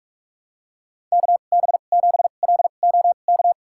Morse code, heard as a keyed tone, about 43 wpm, spaced as for 32 wpm, 700 Hz. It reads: KB7LOK
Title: USA Call Signs - 6 Characters - 2x Character Spacing - 45wpm